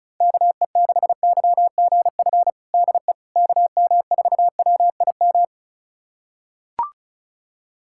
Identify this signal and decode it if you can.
Morse code, heard as a keyed tone, about 36 wpm, 700 Hz. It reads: KE6YGF DE KM4WIM
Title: USA Callsign DE Callsign - 6 Characters - 35wpm